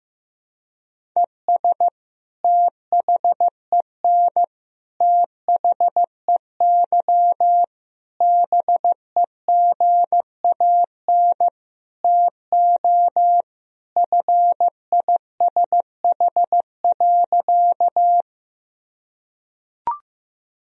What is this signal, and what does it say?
Morse code, heard as a keyed tone, about 15 wpm, 700 Hz. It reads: ES THEN THEY BEGAN TO FISH.
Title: Sentences from Top 300 Words and Abbreviations 15wpm